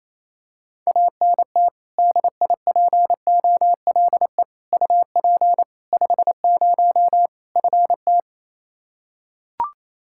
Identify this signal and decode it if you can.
Morse code, heard as a keyed tone, about 28 wpm, 700 Hz. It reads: ANT DIPOLE UP 50 FT
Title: Practice QSO - Exchange 2 - Info - 28wpm